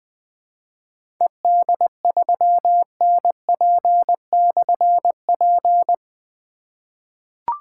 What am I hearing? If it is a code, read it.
Morse code, heard as a keyed tone, about 20 wpm, 700 Hz. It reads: ED3NP/P
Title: Difficult Call Signs 20wpm